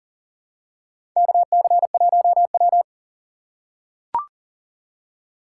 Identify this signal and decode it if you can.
Morse code, heard as a keyed tone, about 40 wpm, 700 Hz. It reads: KC1W
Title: USA Call Signs - 4 Characters - 3x Character Spacing - 40wpm